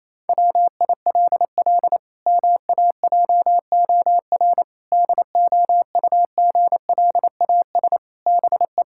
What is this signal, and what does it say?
Morse code, heard as a keyed tone, about 28 wpm, 700 Hz. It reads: WILL MAJOR DOUGLAS BE
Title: Pangram Challenge v2 - 28wpm